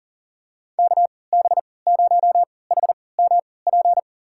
Morse code, heard as a keyed tone, about 40 wpm, spaced as for 23 wpm, 700 Hz. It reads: KB0HMP
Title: USA Call Signs - 3x Character Spacing - 40wpm